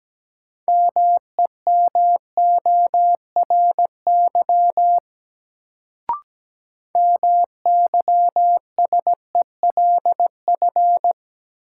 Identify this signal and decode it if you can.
Morse code, heard as a keyed tone, about 17 wpm, 700 Hz. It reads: MEMORY MYSELF
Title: Sentences from Top 1300 Words - New Word Drill - 17wpm